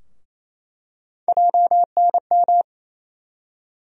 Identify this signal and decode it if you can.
Morse code, heard as a keyed tone, about 28 wpm, 700 Hz. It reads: JNM
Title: USA Call Sign Postfixes - 3 Letters 28wpm